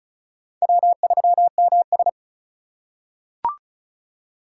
Morse code, heard as a keyed tone, about 36 wpm, 700 Hz. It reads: W3MS
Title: K1USN SST QSO Practice - 35wpm